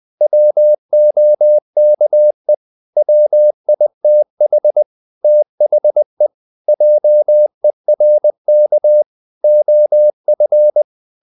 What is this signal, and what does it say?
Morse code, heard as a keyed tone, about 20 wpm, 600 Hz. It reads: WOKE WITH THE JERK OF
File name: the old man and the sea part 2 20wpm